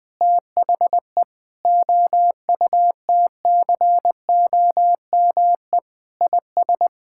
Morse code, heard as a keyed tone, about 21 wpm, 700 Hz. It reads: THE OUTCOME IS
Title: Sentences with 3-Word Phrases - 20wpm